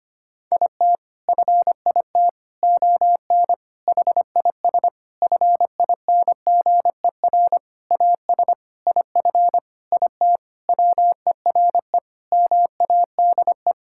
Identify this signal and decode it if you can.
Morse code, heard as a keyed tone, about 25 wpm, 700 Hz. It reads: IT FIT ON HIS FINGER AS IF IT WERE MADE
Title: Sentences from Top 1600 Words 25wpm